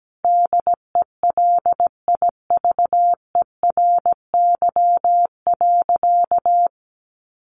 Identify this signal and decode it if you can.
Morse code, heard as a keyed tone, about 17 wpm, 700 Hz. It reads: DELIVERY.
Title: Sentences from Top 1500 Words 17wpm